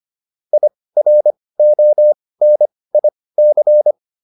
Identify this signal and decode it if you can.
Morse code, heard as a keyed tone, about 25 wpm, spaced as for 18 wpm, 600 Hz. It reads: IRONIC